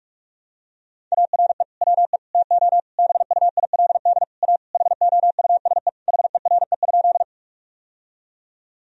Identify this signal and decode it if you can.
Morse code, heard as a keyed tone, about 46 wpm, 700 Hz. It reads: ARE WE TO BUILD A HOUSE HERE?
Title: Sentences from Top 1000 Words - Review Part 1 - 45wpm